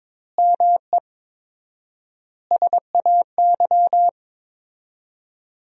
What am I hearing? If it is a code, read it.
Morse code, heard as a keyed tone, about 22 wpm, 700 Hz. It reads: ME SAY
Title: 4x Word-Spacing - Sentences from Top 300 Words - 22wpm